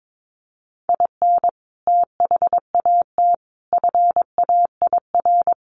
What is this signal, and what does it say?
Morse code, heard as a keyed tone, about 22 wpm, 700 Hz. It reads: IN THAT FAIR
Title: Sentences from Top 1500 Words 22wpm